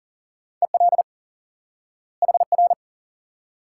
Morse code, heard as a keyed tone, about 40 wpm, 700 Hz. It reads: EL HR
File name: IQE Course 3 - Lesson 89 - YAGI - Two Elements - 40wpm